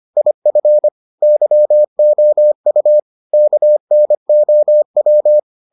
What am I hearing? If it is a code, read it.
Morse code, heard as a keyed tone, about 25 wpm, 600 Hz. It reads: IF YOU KNOW